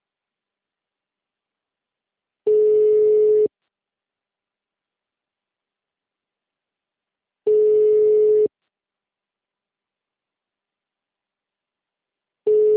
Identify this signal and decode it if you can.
Ringback tone (Europe, 425 Hz).